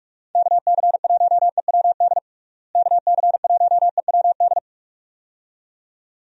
Morse code, heard as a keyed tone, about 41 wpm, 700 Hz. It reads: KC1EWD KC1EWD
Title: USA Call Signs - 6 Characters - Repeated 2x - 45wpm